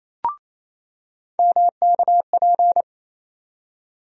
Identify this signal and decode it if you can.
Morse code, heard as a keyed tone, about 28 wpm, 700 Hz. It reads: MKP
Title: USA Call Sign Postfixes - 3 Letters 28wpm